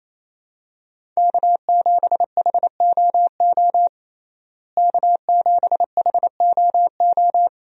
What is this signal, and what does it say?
Morse code, heard as a keyed tone, about 27 wpm, 700 Hz. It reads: K7HOO K7HOO
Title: USA Call Signs - 5 Characters - Repeated 3x - 28wpm